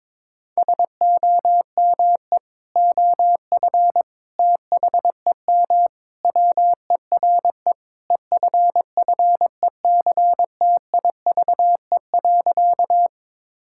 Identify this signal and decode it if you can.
Morse code, heard as a keyed tone, about 22 wpm, 700 Hz. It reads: SOME OF THEM WERE EFFECTIVE.
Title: Sentences from Top 1000 Words - Review Part 6 - 22wpm